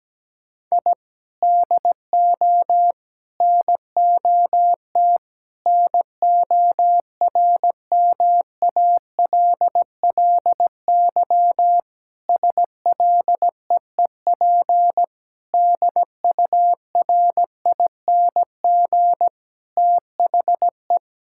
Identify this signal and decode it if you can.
Morse code, heard as a keyed tone, about 17 wpm, 700 Hz. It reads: I DO NOT NORMALLY SLEEP DURING THE
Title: Sentences from Top 1000 Words - Review Part 6 - 17wpm